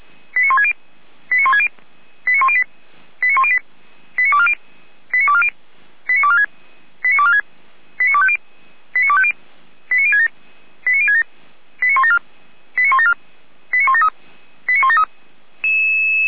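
Five-tone selective call (ZVEI), heard as quick runs of five tones, tones of about 70 ms. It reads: 89160 89160 89108 89108 89240 89240 89246 89246 89250 89250 89067 89067 89174 89174 89173 89173
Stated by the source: ZVEI five-tone selective call received on an SDR.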